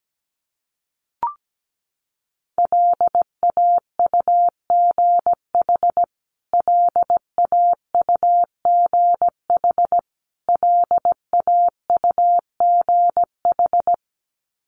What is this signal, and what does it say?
Morse code, heard as a keyed tone, about 17 wpm, 700 Hz. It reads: LAUGH LAUGH LAUGH
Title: Top 500 Words Repeated 17wpm